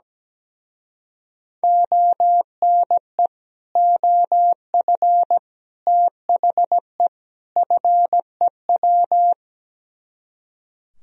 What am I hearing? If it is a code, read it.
Morse code, heard as a keyed tone, about 17 wpm, 700 Hz. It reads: ONE OF THE FEW
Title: Speed-Racing - Sets of 4 Words 17wpm